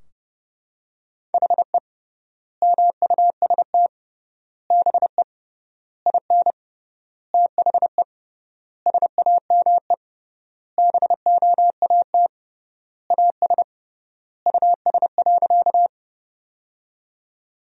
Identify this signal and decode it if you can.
Morse code, heard as a keyed tone, about 30 wpm, 700 Hz. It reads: HE MUST BE IN THE SAME BOAT AS US.
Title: 3x Word-Spacing - Sentences from Top 500 Words - 30wpm